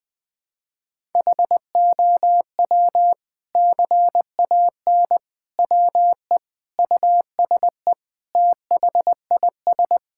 Morse code, heard as a keyed tone, about 20 wpm, 700 Hz. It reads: HOW CAN WE USE THIS
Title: Sentences from Top 1000 Words - Review Part 5 - 20wpm